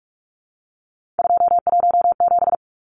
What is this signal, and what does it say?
Morse code, heard as a keyed tone, about 44 wpm, 700 Hz. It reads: 217
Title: Numbers - 3-Digit 45wpm